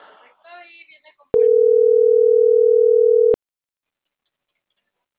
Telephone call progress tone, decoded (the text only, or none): ringback tone